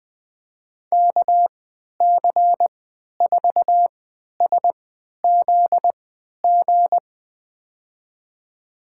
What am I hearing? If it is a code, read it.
Morse code, heard as a keyed tone, about 20 wpm, spaced as for 11 wpm, 700 Hz. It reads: KC4SZG
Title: USA Call Signs - 6 Characters - 3x Character Spacing - 20wpm